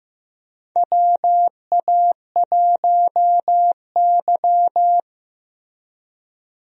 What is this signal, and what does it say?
Morse code, heard as a keyed tone, about 15 wpm, 700 Hz. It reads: WA1Y
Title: USA Call Signs - 4 Characters - Repeated 2x - 15wpm